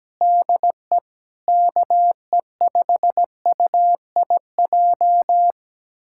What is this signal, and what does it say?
Morse code, heard as a keyed tone, about 17 wpm, 700 Hz. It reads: DE KE5UIJ
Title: USA Callsign DE Callsign 17wpm